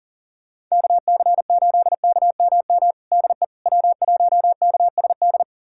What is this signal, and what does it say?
Morse code, heard as a keyed tone, about 40 wpm, 700 Hz. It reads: KC8KMM DE W1KSD
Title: USA Callsign DE Callsign 40wpm